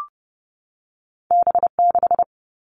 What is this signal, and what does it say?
Morse code, heard as a keyed tone, about 30 wpm, 700 Hz. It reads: B6
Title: All Cut Numbers - 2-Digit 30wpm